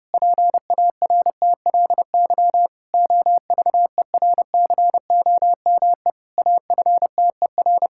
Morse code, heard as a keyed tone, about 30 wpm, 700 Hz. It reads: PARTLY OVERCOME AFTER
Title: Pangram Challenge v2 - 30wpm